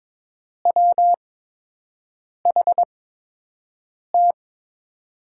Morse code, heard as a keyed tone, about 22 wpm, spaced as for 6 wpm, 700 Hz. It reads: WHT
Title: USA Call Signs - 6 Characters - 8x Character Spacing - 22wpm